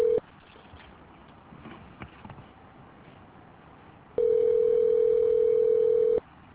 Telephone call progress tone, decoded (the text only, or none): ringback tone